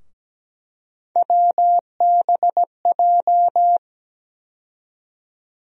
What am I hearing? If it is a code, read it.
Morse code, heard as a keyed tone, about 17 wpm, 700 Hz. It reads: WBJ
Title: USA Call Sign Postfixes - 3 Letters 17wpm